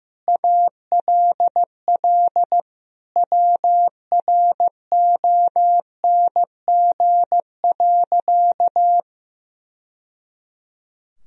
Morse code, heard as a keyed tone, about 15 wpm, 700 Hz. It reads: ALL WRONG.
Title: Sentences from Top 1000 Words - Review Part 3 - 15wpm